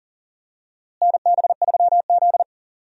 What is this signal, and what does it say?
Morse code, heard as a keyed tone, about 40 wpm, 700 Hz. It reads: NB3Z
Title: USA Call Signs - 4 Characters - Repeated 3x - 40wpm